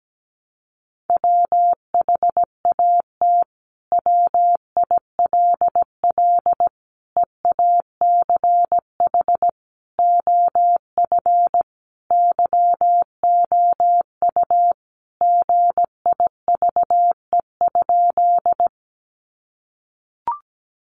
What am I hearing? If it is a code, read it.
Morse code, heard as a keyed tone, about 17 wpm, 700 Hz. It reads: WHAT WILL EACH OF YOU GIVE?
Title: Sentences from Top 200 Words 17wpm